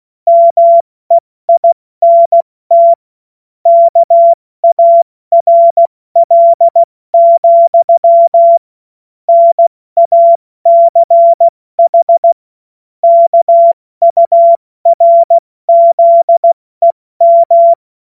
Morse code, heard as a keyed tone, about 16 wpm, 680 Hz. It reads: MEINT KARL, NACH KURZEM
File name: Witz-02-September-2025-70-F80